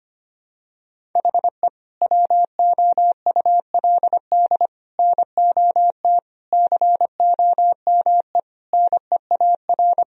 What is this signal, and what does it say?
Morse code, heard as a keyed tone, about 25 wpm, 700 Hz. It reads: HE WOULD NOT COME NEAR
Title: Sentences from Top 1000 Words - Review Part 1 - 25wpm